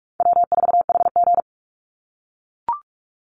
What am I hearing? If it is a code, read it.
Morse code, heard as a keyed tone, about 44 wpm, 700 Hz. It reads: W4HG
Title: Easy Call Signs - 45wpm